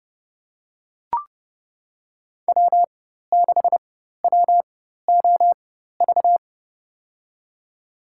Morse code, heard as a keyed tone, about 30 wpm, spaced as for 14 wpm, 700 Hz. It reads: W6WOV
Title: USA Call Signs - 5 Characters - 4x Character Spacing - 30wpm